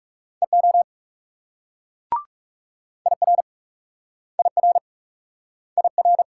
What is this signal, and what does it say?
Morse code, heard as a keyed tone, about 38 wpm, 700 Hz. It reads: EO IR IR IR